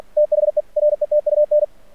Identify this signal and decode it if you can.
Morse code, heard as a keyed tone, about 46 wpm, 600 Hz. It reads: THE RETUN
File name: archived / CRSnet